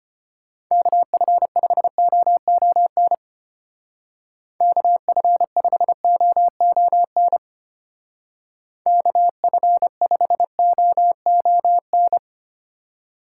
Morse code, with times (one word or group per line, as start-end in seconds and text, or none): 0.7-3.1 KF5OON
4.6-7.4 KF5OON
8.9-12.2 KF5OON